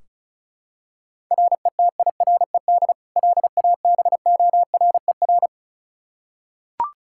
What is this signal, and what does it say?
Morse code, heard as a keyed tone, about 36 wpm, 700 Hz. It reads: RETIRED LABORER
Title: K1USN SST QSO Practice 35wpm